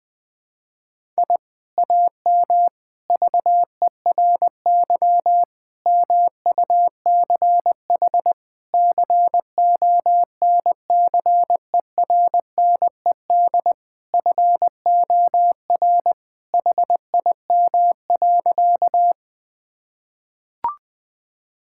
Morse code, heard as a keyed tone, about 20 wpm, 700 Hz. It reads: I AM VERY MUCH CONCERNED FOR HIM.
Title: Sentences from Top 1000 Words - Review Part 6 - 20wpm